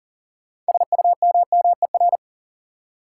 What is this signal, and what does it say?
Morse code, heard as a keyed tone, about 40 wpm, 700 Hz. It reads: SUMMER
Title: Six Letter Words - 40wpm